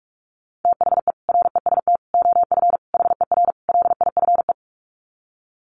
Morse code, heard as a keyed tone, about 44 wpm, 700 Hz. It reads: THE REST OF HER LIFE
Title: Sets of 5 Words Encore 45wpm